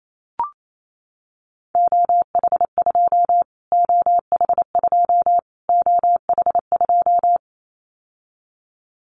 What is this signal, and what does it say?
Morse code, heard as a keyed tone, about 28 wpm, 700 Hz. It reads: OH2 OH2 OH2
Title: Callsign Prefixes for Top 25 DXCC - Repeated 3x - 28wpm